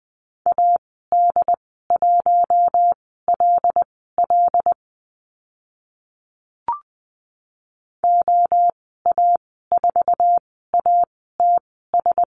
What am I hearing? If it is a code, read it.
Morse code, heard as a keyed tone, about 20 wpm, spaced as for 15 wpm, 700 Hz. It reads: AD1LL OA4ATS
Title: 076 Lesson 13g - Letter L - Call Signs 20wpm